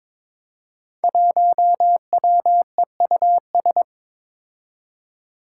Morse code, heard as a keyed tone, about 23 wpm, 700 Hz. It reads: 1WEUS
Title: Five Challenging Letter-Number Combinations 22wpm